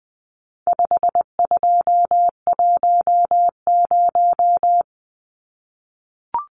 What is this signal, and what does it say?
Morse code, heard as a keyed tone, about 20 wpm, 700 Hz. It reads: 5210
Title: Numbers - 4-Digit 20wpm